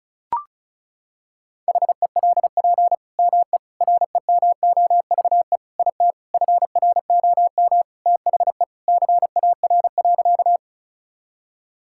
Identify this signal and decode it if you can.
Morse code, heard as a keyed tone, about 36 wpm, 700 Hz. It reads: HELP ME REMOVE IT FROM THE CAR.
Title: Sentences from Top 1000 Words - Review Part 5 - 35wpm